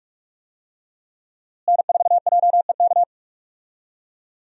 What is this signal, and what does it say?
Morse code, heard as a keyed tone, about 43 wpm, 700 Hz. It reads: N4JEK
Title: USA Call Signs - 5 Characters - Repeated 2x - 45wpm